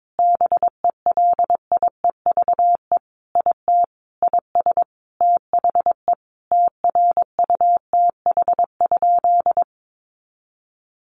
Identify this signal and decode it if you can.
Morse code, heard as a keyed tone, about 22 wpm, 700 Hz. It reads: BELIEVE IT IS THE TRUTH?
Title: Sentences from Top 1000 Words 22wpm